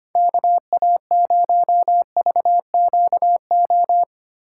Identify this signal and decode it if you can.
Morse code, heard as a keyed tone, about 25 wpm, 700 Hz. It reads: KA0VQO
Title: USA Call Signs - 6 Characters - 2x Character Spacing - 25wpm